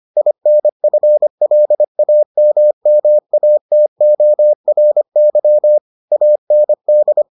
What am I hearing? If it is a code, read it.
Morse code, heard as a keyed tone, about 25 wpm, 600 Hz. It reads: INFLAMMATORY AND